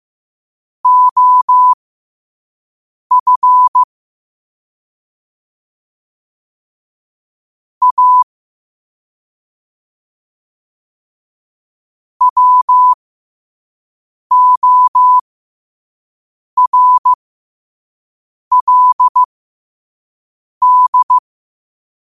Morse code, heard as a keyed tone, about 15 wpm, spaced as for 5 wpm, 1000 Hz. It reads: OF A WORLD
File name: qotdfiles / 5wpm